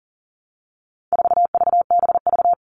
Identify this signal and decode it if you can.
Morse code, heard as a keyed tone, about 40 wpm, 700 Hz. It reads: 4VBV
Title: All Cut Numbers - 4-Digit 40wpm